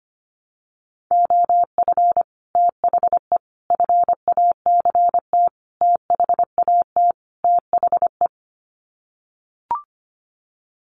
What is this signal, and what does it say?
Morse code, heard as a keyed tone, about 25 wpm, 700 Hz. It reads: OF THE FACT THAT THE
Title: Sets of 5 Words Encore2 25wpm